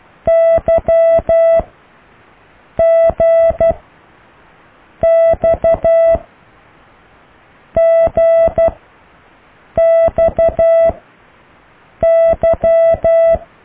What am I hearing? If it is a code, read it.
Morse code, heard as a keyed tone, about 12 wpm, 660 Hz. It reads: Y G X G X Y